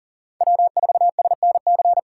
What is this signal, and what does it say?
Morse code, heard as a keyed tone, about 47 wpm, 700 Hz. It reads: W4SNC
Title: Speed Racing - USA 1x3 Call Signs 40wpm